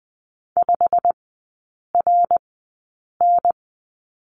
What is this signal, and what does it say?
Morse code, heard as a keyed tone, about 20 wpm, spaced as for 8 wpm, 700 Hz. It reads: 5RN
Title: 107 Lesson 18e - Letter m - Three Characters 20wpm